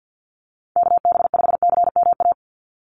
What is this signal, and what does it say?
Morse code, heard as a keyed tone, about 50 wpm, 700 Hz. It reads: KB5CMA
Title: USA Call Signs 50wpm